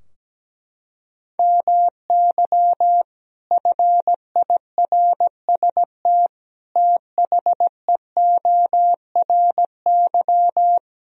Morse code, heard as a keyed tone, about 17 wpm, 700 Hz. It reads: MY FIRST THEORY
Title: Sentences from Top 1000 Words - Review Part 3 - 17wpm